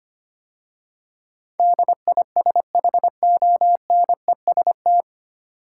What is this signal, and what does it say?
Morse code, heard as a keyed tone, about 25 wpm, 700 Hz. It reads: DISHONEST
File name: practice / 25wpm / 9 Letter Words - First 3 Letters Given - 25wpm